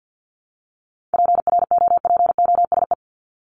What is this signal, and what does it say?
Morse code, heard as a keyed tone, about 49 wpm, 700 Hz. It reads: PROPOSE
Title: ED Suffix - Top 100 Words with Base Words 50wpm